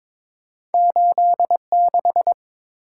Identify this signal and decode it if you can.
Morse code, heard as a keyed tone, about 22 wpm, 700 Hz. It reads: 86
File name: practice / 22wpm / Two Letter-Number Combinations - 22wpm